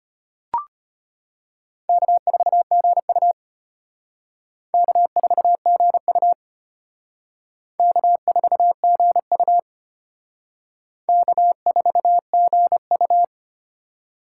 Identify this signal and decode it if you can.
Morse code, heard as a keyed tone, about 32 wpm, 700 Hz. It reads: K4GU K4GU K4GU K4GU